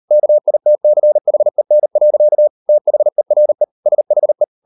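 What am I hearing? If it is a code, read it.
Morse code, heard as a keyed tone, about 38 wpm, 600 Hz. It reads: KITCHEN. THERE SHE
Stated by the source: Morse code practice at 38 WPM, sending Ken Follett's novel Never in English, from CW med Gnister.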